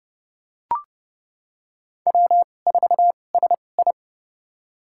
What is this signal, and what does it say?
Morse code, heard as a keyed tone, about 29 wpm, spaced as for 22 wpm, 700 Hz. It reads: W4SI
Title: USA Call Signs - 4 Characters - 2x Character Spacing - 30wpm